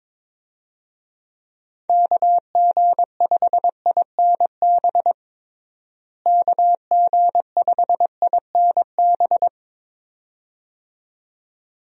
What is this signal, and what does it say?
Morse code, heard as a keyed tone, about 22 wpm, 700 Hz. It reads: KG5INB KG5INB
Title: USA Call Signs - 6 Characters - Repeated 2x - 22wpm